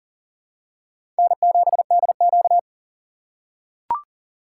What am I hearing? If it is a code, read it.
Morse code, heard as a keyed tone, about 40 wpm, 700 Hz. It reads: N7DQ
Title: USA Call Signs - 4 Characters - 3x Character Spacing - 40wpm